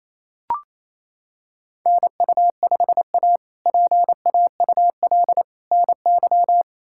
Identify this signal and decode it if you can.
Morse code, heard as a keyed tone, about 28 wpm, 700 Hz. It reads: NU5A PAUL NY
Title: K1USN SST QSO Practice - 28wpm